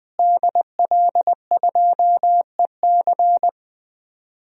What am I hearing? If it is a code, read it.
Morse code, heard as a keyed tone, about 20 wpm, 700 Hz. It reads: DL2EC
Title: DXCC-100 Call Signs - 5 Characters - Repeated 3x - 20wpm